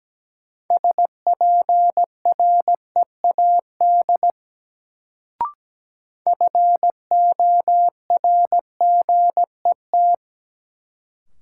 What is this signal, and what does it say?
Morse code, heard as a keyed tone, about 17 wpm, 700 Hz. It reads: SPREAD FORGET
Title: Sentences from Top 1200 Words - New Word Drill - 17wpm